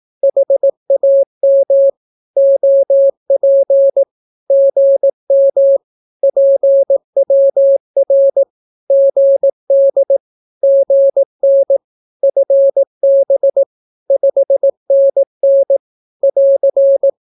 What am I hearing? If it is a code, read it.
Morse code, heard as a keyed tone, about 18 wpm, 550 Hz. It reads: HAM OP GM PWR GD GN FB 5NN <AR>
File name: naukapodstaw / skroty / skroty